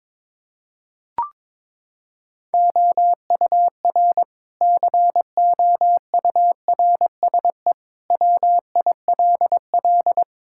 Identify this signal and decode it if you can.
Morse code, heard as a keyed tone, about 22 wpm, 700 Hz. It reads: OUR COURSE WILL
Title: Sentences from Top 1000 Words - Review Part 6 - 22wpm